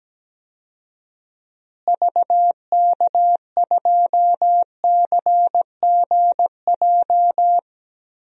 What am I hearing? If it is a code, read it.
Morse code, heard as a keyed tone, about 17 wpm, 700 Hz. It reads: VK2CGJ
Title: DXCC-100 Call Signs - 6 Characters - Repeated 3x - 17wpm